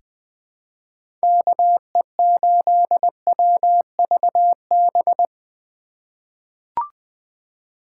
Morse code, heard as a keyed tone, about 21 wpm, 700 Hz. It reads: KE8WVB
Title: 2023 POTA Activator Call Signs - 20wpm